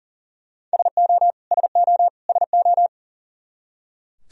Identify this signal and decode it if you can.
Morse code, heard as a keyed tone, about 40 wpm, 700 Hz. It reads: SO SO SO
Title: Top 100 Words Repeated 40wpm